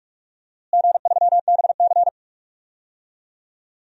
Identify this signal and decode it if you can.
Morse code, heard as a keyed tone, about 41 wpm, 700 Hz. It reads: G3BC
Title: DXCC-100 Call Signs - 4 Characters - Repeated 3x - 45wpm